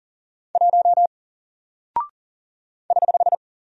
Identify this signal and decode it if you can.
Morse code, heard as a keyed tone, about 40 wpm, 700 Hz. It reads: 1 <HH>
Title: Single Character (Letter:Number:Punctuation:Prosign) Alt 40wpm